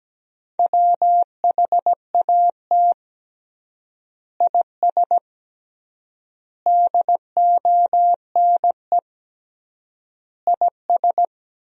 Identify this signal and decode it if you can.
Morse code, heard as a keyed tone, about 17 wpm, 700 Hz. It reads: WHAT IS DONE IS
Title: 3x Word-Spacing - Sentences from Top 500 Words - 17wpm